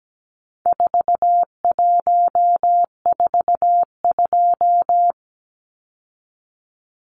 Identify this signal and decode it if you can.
Morse code, heard as a keyed tone, about 17 wpm, 700 Hz. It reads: 4142
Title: Numbers - 4-Digit 17wpm